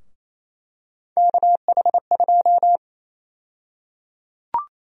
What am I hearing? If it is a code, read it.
Morse code, heard as a keyed tone, about 28 wpm, 700 Hz. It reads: KH2
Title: USA Call Sign Prefixes - 28wpm